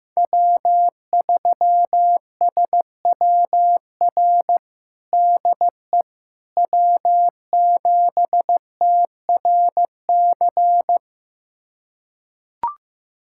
Morse code, heard as a keyed tone, about 16 wpm, 700 Hz. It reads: W3SWR DE W7TRC
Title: USA Callsign DE Callsign - 5 Characters - 15wpm